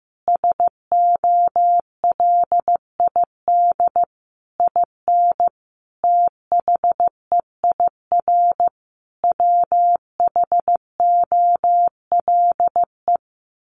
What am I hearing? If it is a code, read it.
Morse code, heard as a keyed tone, about 15 wpm, 700 Hz. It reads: SOLID IN THEIR WHOLE
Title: Sentences from Top 1500 Words 15wpm